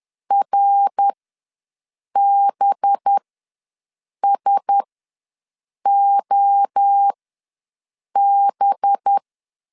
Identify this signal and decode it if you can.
Morse code, heard as a keyed tone, about 10 wpm, spaced as for 6 wpm, 800 Hz. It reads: RBSOB
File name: LEZIONI / 5 LBVC / 5 LB ws 10wpm 3616